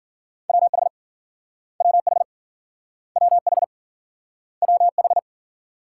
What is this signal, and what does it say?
Morse code, heard as a keyed tone, about 58 wpm, 700 Hz. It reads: WH WH WH WH